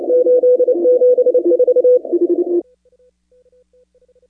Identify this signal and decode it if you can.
Morse code, heard as a keyed tone, about 30 wpm, 500 Hz. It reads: 874
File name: contests / 2019-cw-wpx-kt5j-k5pi- -2330 0-25